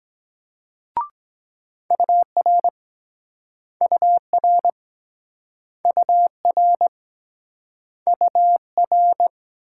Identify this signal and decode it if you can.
Morse code, heard as a keyed tone, about 24 wpm, 700 Hz. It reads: UR UR UR UR